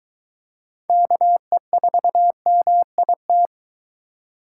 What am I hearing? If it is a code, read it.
Morse code, heard as a keyed tone, about 21 wpm, 700 Hz. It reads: KE4MIT